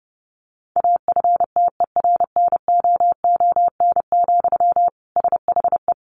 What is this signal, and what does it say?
Morse code, heard as a keyed tone, about 30 wpm, 700 Hz. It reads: AFTERNOON, SHE
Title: Sentences from Top 1500 Words 30wpm